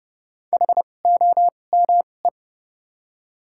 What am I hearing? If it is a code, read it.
Morse code, heard as a keyed tone, about 29 wpm, spaced as for 21 wpm, 700 Hz. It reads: HOME
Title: 2x Character Spacing - Top 300 Words - 30wpm